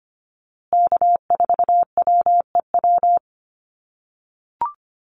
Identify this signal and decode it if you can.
Morse code, heard as a keyed tone, about 25 wpm, 700 Hz. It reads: K4WEW
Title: USA Call Signs Repeated 2x 25wpm